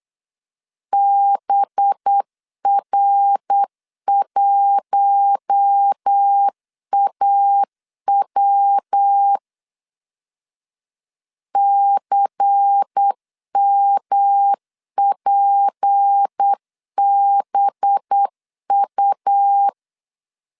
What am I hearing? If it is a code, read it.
Morse code, heard as a keyed tone, about 8 wpm, 800 Hz. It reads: BR1AW CMPBU
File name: LEZIONI / 8 YJ45 / 8 YJ 8wpm 3313